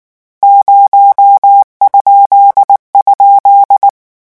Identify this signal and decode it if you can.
Morse code, heard as a keyed tone, about 19 wpm, 800 Hz. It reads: 0??
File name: morse mp3 / lecon13-vitesse 18